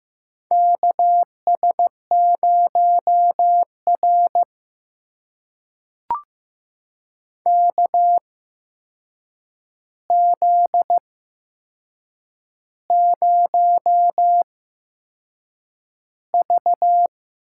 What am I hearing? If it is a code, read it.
Morse code, heard as a keyed tone, about 15 wpm, spaced as for 6 wpm, 700 Hz. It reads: KS0R KZ0V
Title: USA Call Signs - 4 Characters - 8x Character Spacing - 15wpm